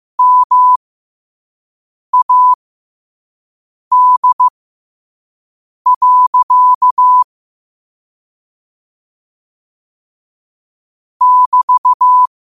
Morse code, heard as a keyed tone, about 15 wpm, spaced as for 5 wpm, 1000 Hz. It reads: MAD. <BT>
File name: qotdfiles / 5wpm